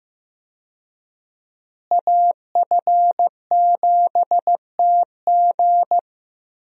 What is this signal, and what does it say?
Morse code, heard as a keyed tone, about 15 wpm, 700 Hz. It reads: AF7TG
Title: USA Call Signs - 5 Characters - Repeated 3x - 15wpm